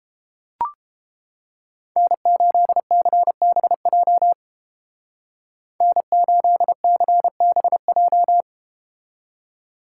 Morse code, heard as a keyed tone, about 31 wpm, 700 Hz. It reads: N8CBJ N8CBJ